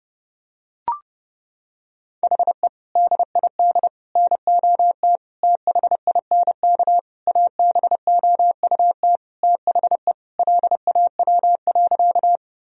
Morse code, heard as a keyed tone, about 32 wpm, 700 Hz. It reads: HE DID NOT THINK ABOUT THE LAW.
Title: Sentences from Top 1000 Words - Review Part 3 - 30wpm